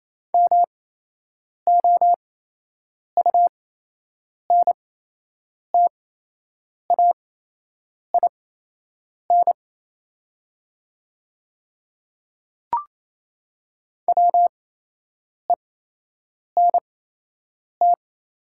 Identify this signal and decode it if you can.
Morse code, heard as a keyed tone, about 28 wpm, spaced as for 8 wpm, 700 Hz. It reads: MOUNTAIN WENT
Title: 8x Character Spacing - Top 500 Words - 28wpm